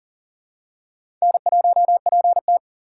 Morse code, heard as a keyed tone, about 37 wpm, 700 Hz. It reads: N1PT
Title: USA Call Signs - 4 Characters - 2x Character Spacing - 40wpm